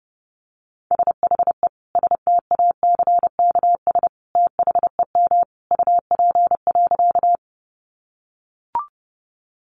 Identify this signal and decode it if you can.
Morse code, heard as a keyed tone, about 30 wpm, 700 Hz. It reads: SHE STACKS THEM UP.
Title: Easy Phrases 30wpm